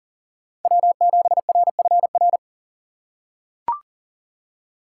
Morse code, heard as a keyed tone, about 40 wpm, 700 Hz. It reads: W7RFR
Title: USA Call Signs - 5 Characters - 2x Character Spacing - 40wpm